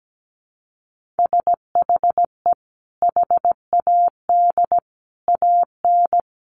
Morse code, heard as a keyed tone, about 17 wpm, 700 Hz. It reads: SHE HAD AN
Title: Sentences from Top 1400 Words 17wpm